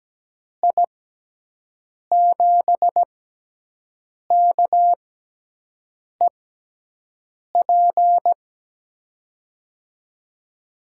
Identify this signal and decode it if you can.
Morse code, heard as a keyed tone, about 17 wpm, 700 Hz. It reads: I 7 K E P
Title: USA Call Signs - 6 Characters - 6x Character Spacing - 17wpm